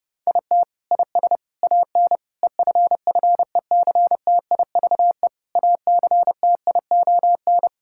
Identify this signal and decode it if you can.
Morse code, heard as a keyed tone, about 32 wpm, 700 Hz. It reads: IT IS AN EFFECTIVE ACTION
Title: Sentences from Top 1000 Words - Review Part 6 - 30wpm